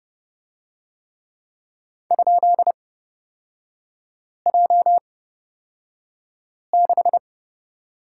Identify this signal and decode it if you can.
Morse code, heard as a keyed tone, about 30 wpm, spaced as for 5 wpm, 700 Hz. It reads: ?J6
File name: Lesson 37e - Letter x - Three Characters 30wpm